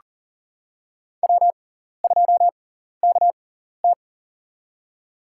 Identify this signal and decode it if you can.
Morse code, heard as a keyed tone, about 40 wpm, spaced as for 14 wpm, 700 Hz. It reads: W2KT
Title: USA Call Signs - 6x Character Spacing - 40wpm